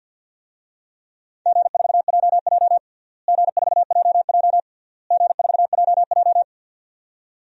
Morse code, heard as a keyed tone, about 48 wpm, 700 Hz. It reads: G4JJ G4JJ G4JJ
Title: DXCC-100 Call Signs - 4 Characters - Repeated 3x - 50wpm